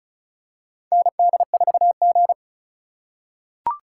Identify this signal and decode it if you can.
Morse code, heard as a keyed tone, about 36 wpm, 700 Hz. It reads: ND4G
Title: K1USN SST QSO Practice - 35wpm